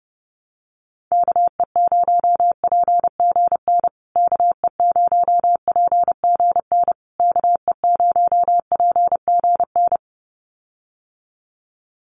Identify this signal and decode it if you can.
Morse code, heard as a keyed tone, about 30 wpm, 700 Hz. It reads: KE0PGN KE0PGN KE0PGN
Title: USA Call Signs Repeated 3x 30wpm